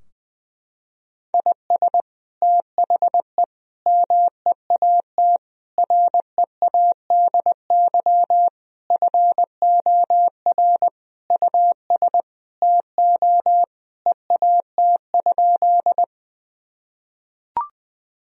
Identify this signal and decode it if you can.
Morse code, heard as a keyed tone, about 20 wpm, 700 Hz. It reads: IS THE MEAT READY FOR US TO EAT?
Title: Sentences from Top 1000 Words - Review Part 3 - 20wpm